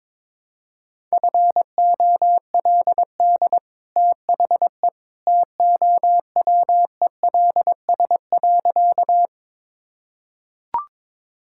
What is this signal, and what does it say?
Morse code, heard as a keyed tone, about 22 wpm, 700 Hz. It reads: FOLD THE TOWELS.
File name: practice / 22wpm / Easy Sentences with Hints - 22wpm